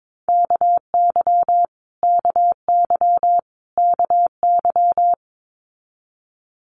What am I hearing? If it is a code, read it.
Morse code, heard as a keyed tone, about 22 wpm, 700 Hz. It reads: KY KY KY
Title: Multi-letter Phonemes Repeated 3x 22wpm